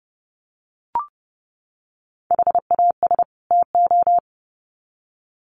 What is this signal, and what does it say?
Morse code, heard as a keyed tone, about 30 wpm, 700 Hz. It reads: HAS TO
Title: Sets of 2 Words 30wpm